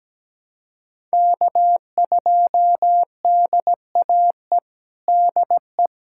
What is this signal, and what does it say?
Morse code, heard as a keyed tone, about 17 wpm, 700 Hz. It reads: K2DAE DE
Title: USA Callsign DE Callsign 17wpm